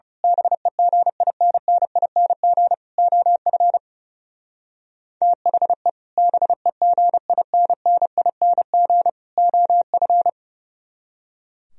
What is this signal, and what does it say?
Morse code, heard as a keyed tone, about 32 wpm, 700 Hz. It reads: BEGINNING OF THE BEGINNING OF
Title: Speed-Racing - Sets of 3 Words 30wpm